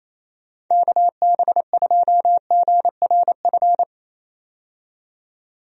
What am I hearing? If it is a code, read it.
Morse code, heard as a keyed tone, about 28 wpm, 700 Hz. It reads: KB2GRF
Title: USA Call Signs - 6 Characters - 28wpm